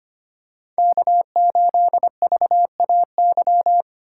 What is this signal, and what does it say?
Morse code, heard as a keyed tone, about 25 wpm, 700 Hz. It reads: K8VAY
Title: USA Call Signs - 3x Character Spacing - 25wpm